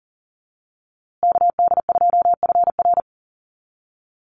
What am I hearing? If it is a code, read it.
Morse code, heard as a keyed tone, about 40 wpm, 700 Hz. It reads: KD2FR
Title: USA Call Signs Repeated 2x 40wpm